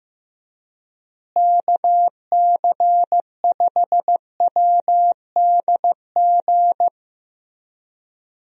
Morse code, heard as a keyed tone, about 15 wpm, 700 Hz. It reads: KC5WDG
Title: USA Call Signs - 6 Characters - Repeated 3x - 15wpm